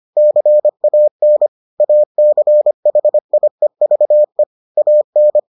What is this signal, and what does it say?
Morse code, heard as a keyed tone, about 25 wpm, 600 Hz. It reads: CAN ACHIEVE AN